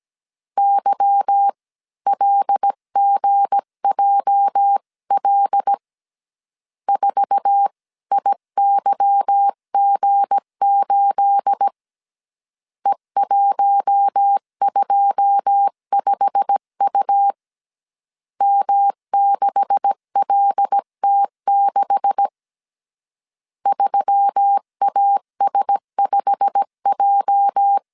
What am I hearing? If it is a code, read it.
Morse code, heard as a keyed tone, about 17 wpm, 800 Hz. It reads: YLGJL 4IYG8 E125U M6LT6 3AS5J
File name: CW / DET / 1 9 1500C S B 16wpm 2856